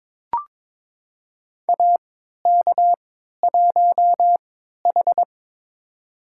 Morse code, heard as a keyed tone, about 21 wpm, spaced as for 12 wpm, 700 Hz. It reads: AK1H
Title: USA Call Signs - 4 Characters - 3x Character Spacing - 22wpm